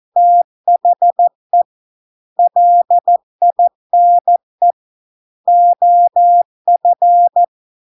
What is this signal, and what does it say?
Morse code, heard as a keyed tone, about 14 wpm, 700 Hz. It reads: THE LINE OF